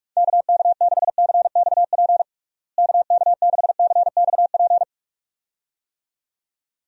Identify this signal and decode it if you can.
Morse code, heard as a keyed tone, about 45 wpm, 700 Hz. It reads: KK6CXP KK6CXP
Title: USA Call Signs - 6 Characters - Repeated 2x - 45wpm